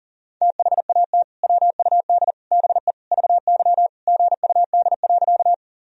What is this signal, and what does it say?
Morse code, heard as a keyed tone, about 40 wpm, 700 Hz. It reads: THAT WUD BE VY GUD.
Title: Sentences from Top 200 Words and Abbreviations 40wpm